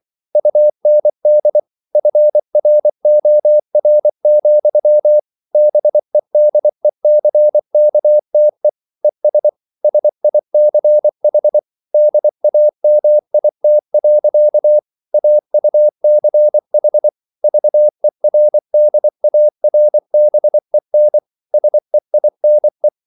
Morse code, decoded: UND FROR, BEDECKTE ES SICH DAMIT. AUCH VERDARBEN SEINE

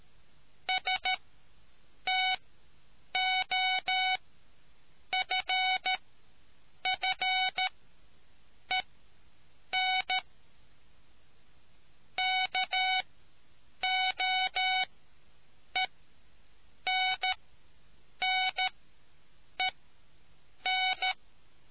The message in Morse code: STOFFEN KOENNEN

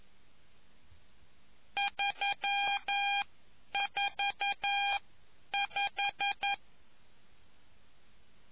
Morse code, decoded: 345

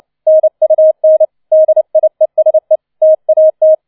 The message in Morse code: NUN DIESE TAT

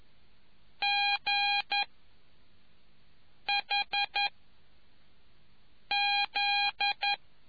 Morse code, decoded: G H Z

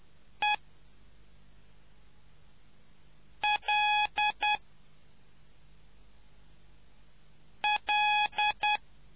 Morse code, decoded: E L L

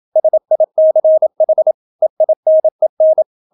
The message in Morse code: SICH EINEN

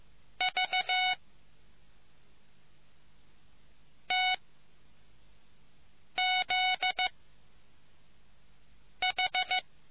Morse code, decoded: V T Z H

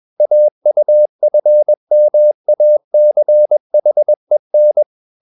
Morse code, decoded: AUFMACHEN

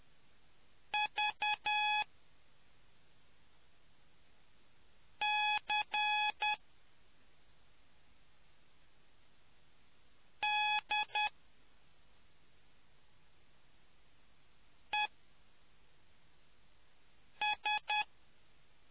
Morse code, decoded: VCDES

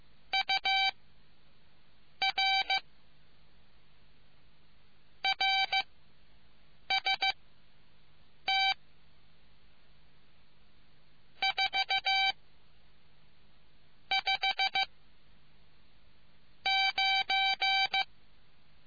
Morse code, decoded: UR RST 4 5 9

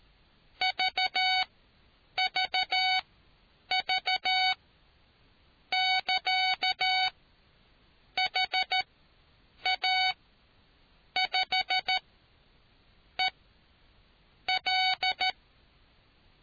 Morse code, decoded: VVV <KA> HA5EL

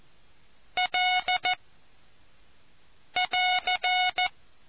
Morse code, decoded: L <AR>

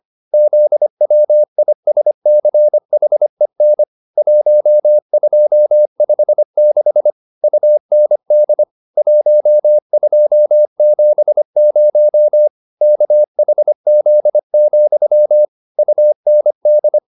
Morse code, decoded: ZWISCHEN 1256 UND 1270 KHZ, UND

